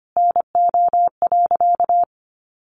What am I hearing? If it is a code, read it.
Morse code, heard as a keyed tone, about 25 wpm, 700 Hz. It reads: NO.
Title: Sentences from Top 500 Words 25wpm